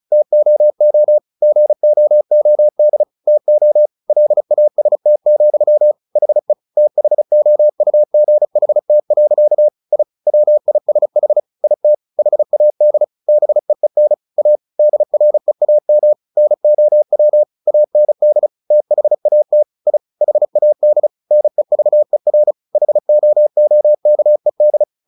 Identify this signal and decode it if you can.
Morse code, heard as a keyed tone, about 34 wpm, 600 Hz. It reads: TOO GOOD TO LAST, HE THOUGHT. I WISH IT HAD BEEN A DREAM NOW AND THAT I HAD NEVER HOOKED
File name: the old man and the sea 2 35wpm